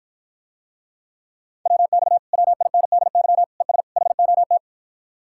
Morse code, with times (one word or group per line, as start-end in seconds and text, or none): 1.7-2.2 WX
2.3-3.4 WINDY
3.6-3.8 ES
4.0-4.6 HOT